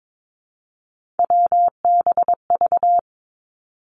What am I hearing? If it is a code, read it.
Morse code, heard as a keyed tone, about 22 wpm, 700 Hz. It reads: WBV
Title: Three Challenging Letter-Number Combinations 22wpm